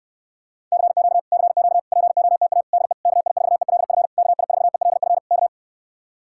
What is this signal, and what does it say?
Morse code, heard as a keyed tone, about 69 wpm, 700 Hz. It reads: CQ CQ POTA DE KI4ELF KI4ELF K